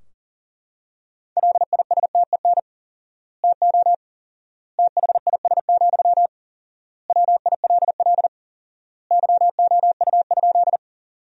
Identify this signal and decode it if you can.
Morse code, heard as a keyed tone, about 40 wpm, 700 Hz. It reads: LISTEN TO THIS, WILL YOU?
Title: 4x Word-Spacing - Sentences from Top 400 Words - 40wpm